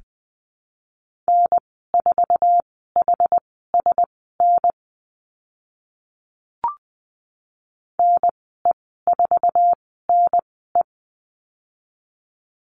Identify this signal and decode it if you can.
Morse code, heard as a keyed tone, about 20 wpm, spaced as for 14 wpm, 700 Hz. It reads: N4HSN NE4NE
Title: 082 Lesson 14g - Number 2 - Call Signs 20wpm